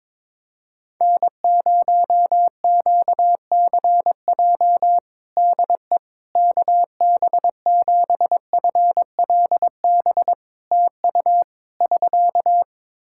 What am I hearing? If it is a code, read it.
Morse code, heard as a keyed tone, about 22 wpm, 700 Hz. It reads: N0QCJ DE KB7FLB TU <SK>
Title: Practice QSO - Exchange 4 - Ending - 22wpm